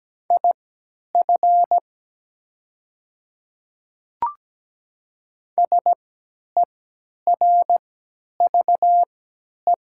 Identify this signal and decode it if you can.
Morse code, heard as a keyed tone, about 17 wpm, spaced as for 10 wpm, 700 Hz. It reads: IF SERVE
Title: 3x Character Spacing - Top 500 Words - 17wpm